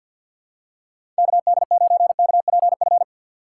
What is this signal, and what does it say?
Morse code, heard as a keyed tone, about 50 wpm, 700 Hz. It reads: KD9KPF
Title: USA Call Signs - 6 Characters - 8x Character Spacing - 50wpm